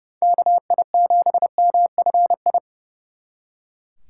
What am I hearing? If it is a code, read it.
Morse code, heard as a keyed tone, about 38 wpm, 700 Hz. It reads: KI7MFI